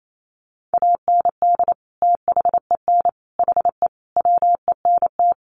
Morse code, heard as a keyed tone, about 28 wpm, 700 Hz. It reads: AND THEN HE WENT
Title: Sentences from Top 1100 Words 28wpm